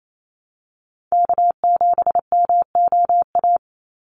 Morse code, heard as a keyed tone, about 28 wpm, 700 Hz. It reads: K7MOA
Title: Call Signs Repeated 2x - 28wpm